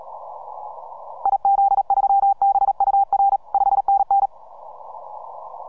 Morse code, heard as a keyed tone, about 38 wpm, 790 Hz. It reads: IZ3BUR 5NN